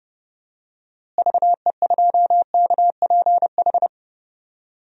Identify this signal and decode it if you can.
Morse code, heard as a keyed tone, about 30 wpm, 700 Hz. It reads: VE2KPH
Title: DXCC-100 Call Signs - 6 Characters - Repeated 2x - 30wpm